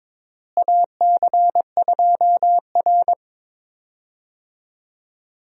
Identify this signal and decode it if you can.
Morse code, heard as a keyed tone, about 22 wpm, 700 Hz. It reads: AC2R
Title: USA Call Signs - 4 Characters - Repeated 2x - 22wpm